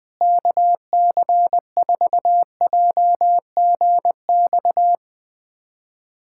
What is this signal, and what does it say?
Morse code, heard as a keyed tone, about 20 wpm, 700 Hz. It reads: KC4JGX
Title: USA Callsign DE Callsign - 6 Characters - 20wpm